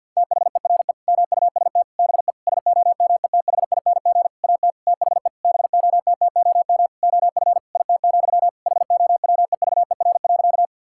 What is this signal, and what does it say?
Morse code, heard as a keyed tone, about 47 wpm, 700 Hz. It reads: THERE MUST BE SOMETHING AT THE BOTTOM OF IT, HOWEVER.
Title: Sentences from Top 1000 Words - Review Part 6 - 50wpm